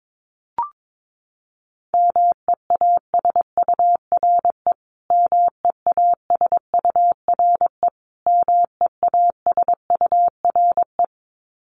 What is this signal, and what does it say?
Morse code, heard as a keyed tone, about 22 wpm, 700 Hz. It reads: MEASURE MEASURE MEASURE
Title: Top 500 Words Repeated 22wpm